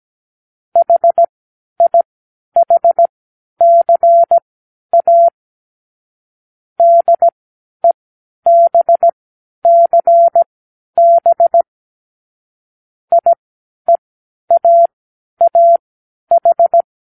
Morse code, heard as a keyed tone, about 17 wpm, spaced as for 10 wpm, 680 Hz. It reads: HIHCA DEBCB IEAAH
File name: k7qo / Lesson15(A-J)